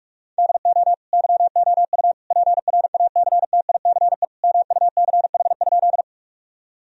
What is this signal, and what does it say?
Morse code, heard as a keyed tone, about 46 wpm, 700 Hz. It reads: DO YOU PRACTICE MUCH?
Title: Sentences from Top 1000 Words - Review Part 4 - 45wpm